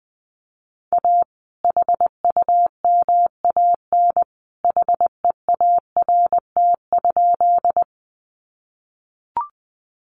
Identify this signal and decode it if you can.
Morse code, heard as a keyed tone, about 19 wpm, 700 Hz. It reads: A HUMAN HEART?
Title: Sentences from Top 1200 Words 20wpm